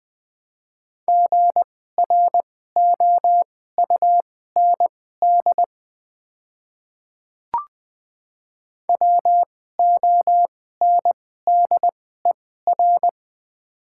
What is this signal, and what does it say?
Morse code, heard as a keyed tone, about 20 wpm, spaced as for 14 wpm, 690 Hz. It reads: GROUND WONDER
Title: 2x Character Spacing - Top 500 Words - 20wpm